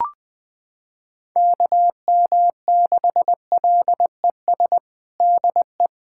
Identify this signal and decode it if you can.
Morse code, heard as a keyed tone, about 19 wpm, 700 Hz. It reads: KM6LES DE
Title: USA Callsign DE Callsign - 6 Characters - 20wpm